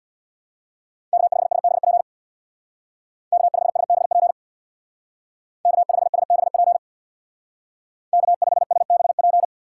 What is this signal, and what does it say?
Morse code, heard as a keyed tone, about 61 wpm, 700 Hz. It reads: K5SBP K5SBP K5SBP K5SBP
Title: Speed Racing - USA Call Signs 50wpm